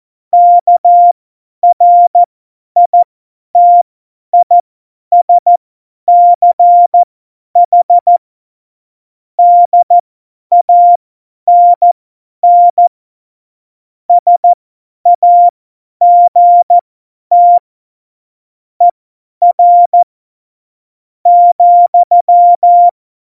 Morse code, heard as a keyed tone, about 14 wpm, spaced as for 10 wpm, 700 Hz. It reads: KRITISCH DANN SAGT ER ,
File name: Morsekurs / Witze / Witz 15. Dezember 2020, 50 F70